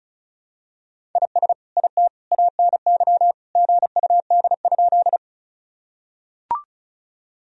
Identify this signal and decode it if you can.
Morse code, heard as a keyed tone, about 36 wpm, 690 Hz. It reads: IS IT ANY GUD?
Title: Sentences from Top 200 Words and Abbreviations 35wpm